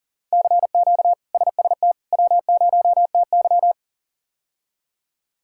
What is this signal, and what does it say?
Morse code, heard as a keyed tone, about 39 wpm, 700 Hz. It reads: CQ SST W0TY
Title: K1USN SST QSO Practice - 40wpm